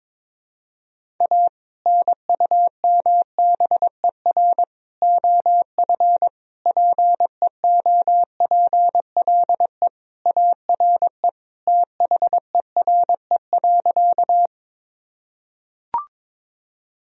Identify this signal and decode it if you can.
Morse code, heard as a keyed tone, about 22 wpm, 700 Hz. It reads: A NUMBER OF PEOPLE ARE THERE.
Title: Sentences from Top 1000 Words - Review Part 1 - 22wpm